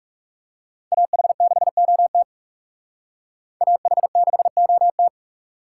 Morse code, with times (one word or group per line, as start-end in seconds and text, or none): 0.9-2.2 AH6OT
3.6-5.1 AH6OT